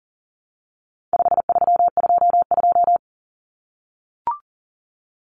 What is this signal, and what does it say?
Morse code, heard as a keyed tone, about 40 wpm, 700 Hz. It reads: 5322
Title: Common Cut Numbers - 4-Digit 40wpm